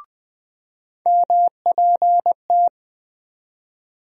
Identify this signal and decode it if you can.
Morse code, heard as a keyed tone, about 20 wpm, 700 Hz. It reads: MPT
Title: USA Call Sign Postfixes - 3 Letters 20wpm